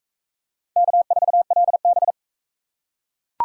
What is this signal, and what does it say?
Morse code, heard as a keyed tone, about 40 wpm, 700 Hz. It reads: K4LB